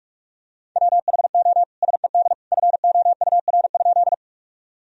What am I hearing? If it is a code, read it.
Morse code, heard as a keyed tone, about 41 wpm, 700 Hz. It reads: WHO SED FOUR?
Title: Sentences from Top 300 Words and Abbreviations 45wpm